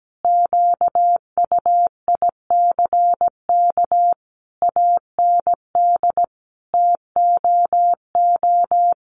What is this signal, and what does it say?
Morse code, heard as a keyed tone, about 17 wpm, 700 Hz. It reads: QUICK AND TOO
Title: Sentences from Top 600 Words 17wpm